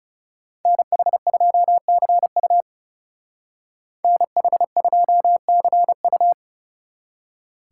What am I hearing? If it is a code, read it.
Morse code, heard as a keyed tone, about 39 wpm, 700 Hz. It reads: NH2CU NH2CU